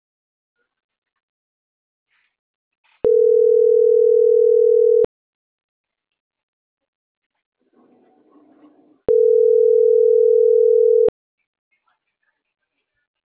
Ringback tone (North America), 440 + 480 Hz.